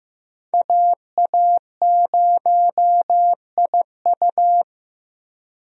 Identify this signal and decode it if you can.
Morse code, heard as a keyed tone, about 15 wpm, 700 Hz. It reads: AA0IU
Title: USA Call Signs - 5 Characters - Repeated 2x - 15wpm